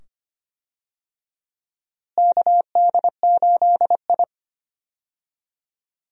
Morse code, heard as a keyed tone, about 25 wpm, 700 Hz. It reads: KD8I
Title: USA Call Signs - 4 Characters - Repeated 2x - 25wpm